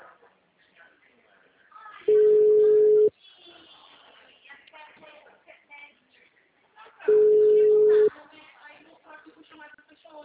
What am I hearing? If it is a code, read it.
Ringback tone (Europe, 425 Hz), with speech.
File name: out- -2013- - - .84030